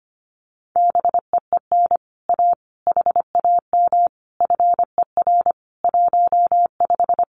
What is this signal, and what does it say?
Morse code, heard as a keyed tone, about 25 wpm, 700 Hz. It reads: BEEN A HAM FER 15
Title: Common QSO Phrases Encore 25wpm